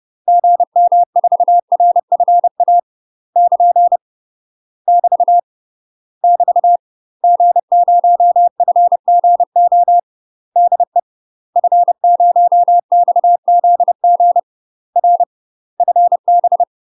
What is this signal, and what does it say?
Morse code, heard as a keyed tone, about 30 wpm, 700 Hz. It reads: GM4RFA <KN> <BT> <BT> G0FGO DE F0XZG R FB
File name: UK MASTER QSOs TRAINING / 30WPM / UK MASTER QSOs 0014